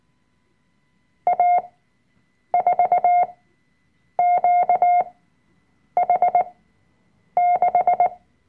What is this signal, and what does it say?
Morse code, heard as a keyed tone, about 19 wpm, spaced as for 7 wpm, 710 Hz. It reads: A4QH6
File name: Workshop Dateien / CW-Online-Kurs / Lektion 005